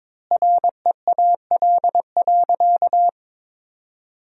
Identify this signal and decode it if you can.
Morse code, heard as a keyed tone, about 22 wpm, 700 Hz. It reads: REAL.
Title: 3x Word-Spacing - Sentences from Top 300 Words - 22wpm